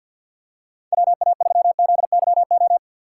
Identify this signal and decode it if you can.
Morse code, heard as a keyed tone, about 47 wpm, 700 Hz. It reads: WA3ZYO
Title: USA Call Signs - 6 Characters - 4x Character Spacing - 50wpm